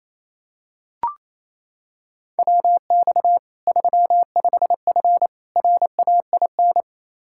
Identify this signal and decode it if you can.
Morse code, heard as a keyed tone, about 28 wpm, 700 Hz. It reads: WX 35F RAIN
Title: K1USN SST QSO Practice 28wpm